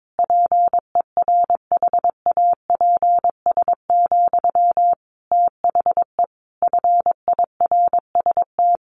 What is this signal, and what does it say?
Morse code, heard as a keyed tone, about 22 wpm, 700 Hz. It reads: PERHAPS, THE FIRST
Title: Sentences from Top 1500 Words 22wpm